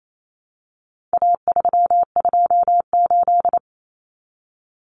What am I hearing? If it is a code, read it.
Morse code, heard as a keyed tone, about 28 wpm, 700 Hz. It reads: A328
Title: Common Cut Numbers - 4-Digit 28wpm